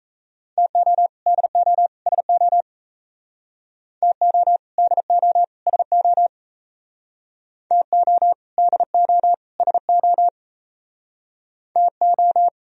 Morse code, heard as a keyed tone, about 37 wpm, 700 Hz. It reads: TO DO SO TO DO SO TO DO SO TO